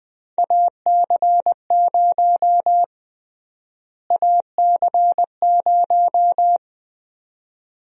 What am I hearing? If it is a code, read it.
Morse code, heard as a keyed tone, about 20 wpm, 700 Hz. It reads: AC0 AC0
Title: USA Call Sign Prefixes - Repeated 2x - 20wpm